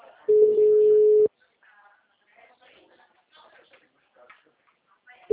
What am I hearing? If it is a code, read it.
Ringback tone (Europe, 425 Hz), with speech.